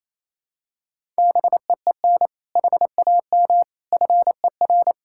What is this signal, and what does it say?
Morse code, heard as a keyed tone, about 28 wpm, 700 Hz. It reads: BEEN HAM FER
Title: K1USN SST QSO Practice 28wpm